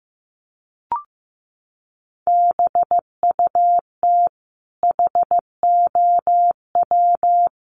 Morse code, heard as a keyed tone, about 14 wpm, 700 Hz. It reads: BUT HOW
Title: Sentences from Top 500 Words 15wpm